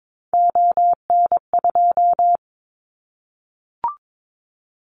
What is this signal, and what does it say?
Morse code, heard as a keyed tone, about 22 wpm, 700 Hz. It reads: ON2
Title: Callsign Prefixes for Top 100 DXCC - 22wpm